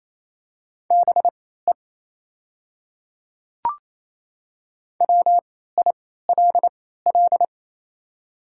Morse code, heard as a keyed tone, about 28 wpm, spaced as for 16 wpm, 700 Hz. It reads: BE WILL
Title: 3x Character Spacing - Top 100 Words - 28wpm